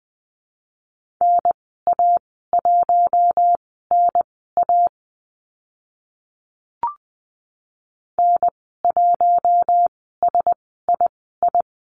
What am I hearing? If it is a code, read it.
Morse code, heard as a keyed tone, about 20 wpm, spaced as for 14 wpm, 700 Hz. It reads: NA1NA N1SII
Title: 042 Lesson 8g - Number 1 - Call Signs 20wpm